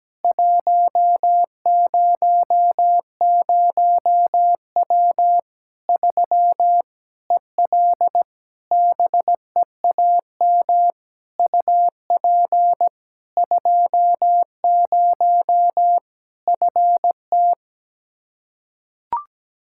Morse code, heard as a keyed tone, about 17 wpm, 700 Hz. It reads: 100W 3 EL BEAM UP 20 FT
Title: Practice QSO - Exchange 2 - Info - 17wpm